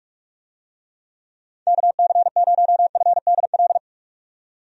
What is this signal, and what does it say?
Morse code, heard as a keyed tone, about 45 wpm, 700 Hz. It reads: KC0FDL
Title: USA Call Signs - 6 Characters - 45wpm